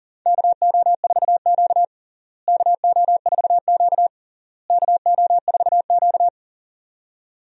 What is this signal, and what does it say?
Morse code, heard as a keyed tone, about 40 wpm, 700 Hz. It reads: KO4Q KO4Q KO4Q
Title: USA Call Signs - 4 Characters - Repeated 3x - 40wpm